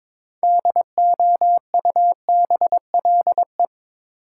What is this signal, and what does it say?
Morse code, heard as a keyed tone, about 22 wpm, 700 Hz. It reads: DOUBLE
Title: Sentences from Top 1400 Words - New Word Drill Alt - 22wpm